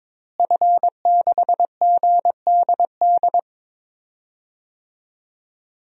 Morse code, heard as a keyed tone, about 22 wpm, 700 Hz. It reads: F6GDD
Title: DXCC-100 Call Signs - 5 Characters - Repeated 3x - 22wpm